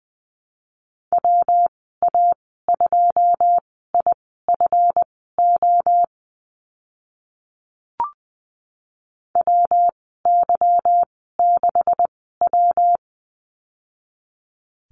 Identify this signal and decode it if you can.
Morse code, heard as a keyed tone, about 20 wpm, spaced as for 14 wpm, 700 Hz. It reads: WA2IFO WY6W
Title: 167 Lesson 27g - Number 7 - Call Signs 20wpm